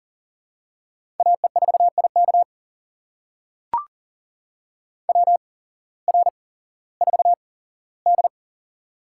Morse code, heard as a keyed tone, about 40 wpm, spaced as for 19 wpm, 700 Hz. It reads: AE4IK WR4D
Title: USA Call Signs - 8x Character Spacing - 40wpm